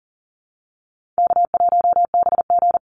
Morse code, heard as a keyed tone, about 40 wpm, 700 Hz. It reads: K1BG
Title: Call Signs - 40wpm